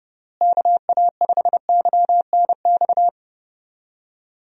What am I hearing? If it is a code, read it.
Morse code, heard as a keyed tone, about 30 wpm, 700 Hz. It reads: KA5YNX
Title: USA Call Signs - 6 Characters - Repeated 2x - 30wpm